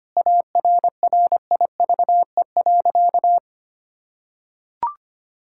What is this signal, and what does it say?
Morse code, heard as a keyed tone, about 25 wpm, 700 Hz. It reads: ARRIVE.
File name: practice / 25wpm / Sentences with 3-Word Phrases - 25wpm